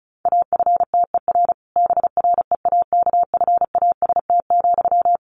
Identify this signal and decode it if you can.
Morse code, heard as a keyed tone, about 34 wpm, 700 Hz. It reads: AFTER BREAKFAST,
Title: Sentences from Top 1500 Words 35wpm